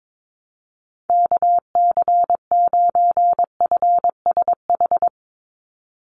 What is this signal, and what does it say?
Morse code, heard as a keyed tone, about 22 wpm, 700 Hz. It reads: KC9FSH
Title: USA Call Signs 22wpm